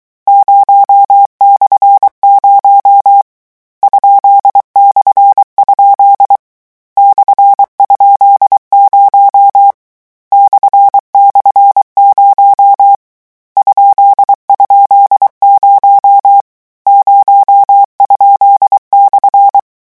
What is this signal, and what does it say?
Morse code, heard as a keyed tone, about 23 wpm, 800 Hz. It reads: 0/0 ?/? /?0 //0 ??0 0?/
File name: morse mp3 / lecon13-vitesse 22